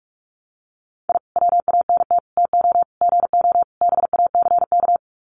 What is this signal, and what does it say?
Morse code, heard as a keyed tone, about 44 wpm, 700 Hz. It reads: I WANT TO GO BACK
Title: Sets of 5 Words Encore 45wpm